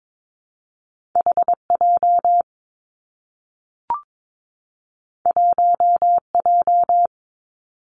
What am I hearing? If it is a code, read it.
Morse code, heard as a keyed tone, about 22 wpm, 700 Hz. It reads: H J 1 J
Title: Two Challenging Letter-Number Combinations 22wpm